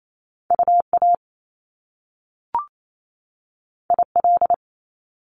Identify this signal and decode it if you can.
Morse code, heard as a keyed tone, about 28 wpm, 700 Hz. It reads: UA IL
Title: Top Two Letter Combinations in Words 28wpm